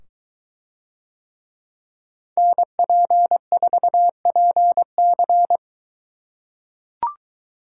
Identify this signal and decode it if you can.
Morse code, heard as a keyed tone, about 22 wpm, 700 Hz. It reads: NP4PC